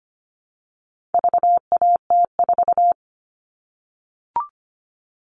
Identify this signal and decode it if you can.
Morse code, heard as a keyed tone, about 25 wpm, 700 Hz. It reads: VAT4
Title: All Cut Numbers - 4-Digit 25wpm